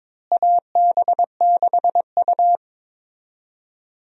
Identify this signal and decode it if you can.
Morse code, heard as a keyed tone, about 22 wpm, 700 Hz. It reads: AB6U
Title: USA Call Signs - 4 Characters - Repeated 3x - 22wpm